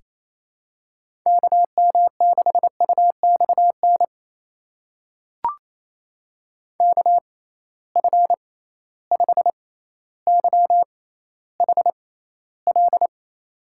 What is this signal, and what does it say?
Morse code, heard as a keyed tone, about 28 wpm, spaced as for 14 wpm, 700 Hz. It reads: KM6UXN KF5YHL
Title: USA Call Signs - 6 Characters - 6x Character Spacing - 28wpm